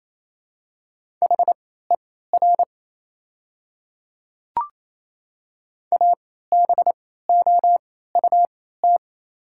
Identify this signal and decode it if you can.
Morse code, heard as a keyed tone, about 28 wpm, spaced as for 16 wpm, 690 Hz. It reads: HER ABOUT
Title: 3x Character Spacing - Top 100 Words - 28wpm